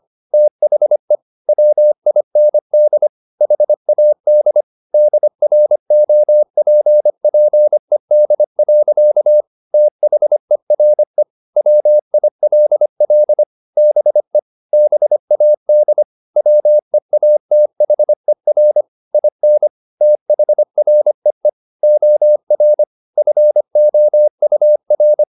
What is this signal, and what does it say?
Morse code, heard as a keyed tone, about 25 wpm, 600 Hz. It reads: THE WIND HAD DROPPED. THERE WILL BE BAD WEATHER IN THREE OR FOUR